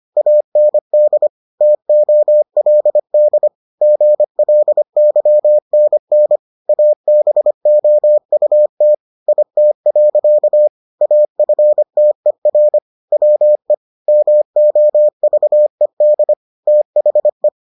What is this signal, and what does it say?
Morse code, heard as a keyed tone, about 25 wpm, 600 Hz. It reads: AND TOLD GLYNN ABOUT IT. AFTER WE MOVED THE